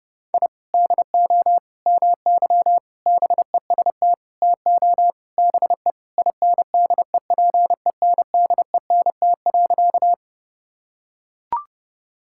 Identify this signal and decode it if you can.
Morse code, heard as a keyed tone, about 30 wpm, 700 Hz. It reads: I DO MY BEST TO BE INDEPENDENT.
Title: Sentences from Top 1000 Words - Review Part 6 - 30wpm